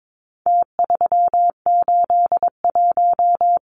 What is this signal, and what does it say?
Morse code, heard as a keyed tone, about 22 wpm, 700 Hz. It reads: T381
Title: Numbers - Contest Style - 4-Digit 22wpm